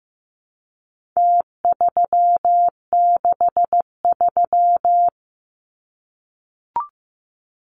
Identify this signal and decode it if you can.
Morse code, heard as a keyed tone, about 15 wpm, 700 Hz. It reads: T363
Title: Numbers - Contest Style - 4-Digit 15wpm